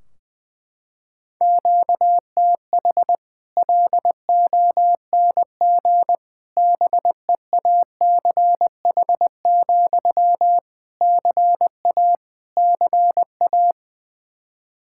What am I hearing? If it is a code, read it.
Morse code, heard as a keyed tone, about 20 wpm, 700 Hz. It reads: QTH LONG BEACH, CA CA
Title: Practice QSO - Exchange 1 - Beginning - 20wpm